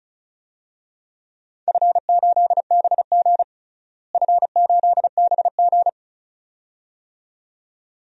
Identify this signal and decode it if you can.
Morse code, heard as a keyed tone, about 34 wpm, 700 Hz. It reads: F8BG F8BG
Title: DXCC-100 Call Signs - 4 Characters - Repeated 2x - 35wpm